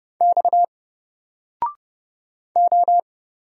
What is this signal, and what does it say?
Morse code, heard as a keyed tone, about 30 wpm, 700 Hz. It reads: X O
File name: Single-Letter-Number-30wpm